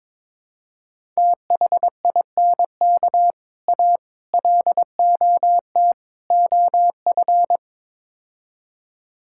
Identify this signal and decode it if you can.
Morse code, heard as a keyed tone, about 22 wpm, 700 Hz. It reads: THINK A LOT OF
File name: practice / 22wpm / Build A Phrase - 5-Word Phrases - 22wpm